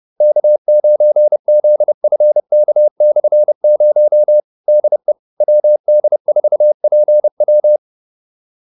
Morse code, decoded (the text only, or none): K9ZFK/0 DE WD4PW